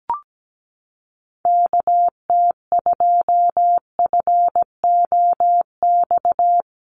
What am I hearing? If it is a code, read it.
Morse code, heard as a keyed tone, about 17 wpm, 700 Hz. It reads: KT2FOX
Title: USA Call Signs 17wpm